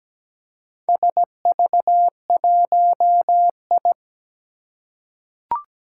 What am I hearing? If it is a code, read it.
Morse code, heard as a keyed tone, about 17 wpm, 700 Hz. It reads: SV1I
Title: Four Challenging Letter-Number Combinations 17wpm